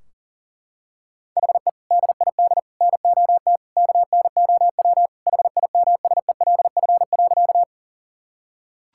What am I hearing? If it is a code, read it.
Morse code, heard as a keyed tone, about 37 wpm, 700 Hz. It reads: HE DID NOT KNOW HIMSELF.
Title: Sentences from Top 1600 Words 40wpm